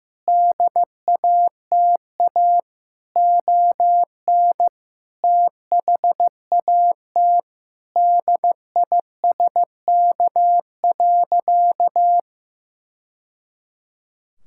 Morse code, heard as a keyed tone, about 15 wpm, 700 Hz. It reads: DATA ON THAT DISK.
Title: Sentences from Top 1000 Words - Review Part 6 - 15wpm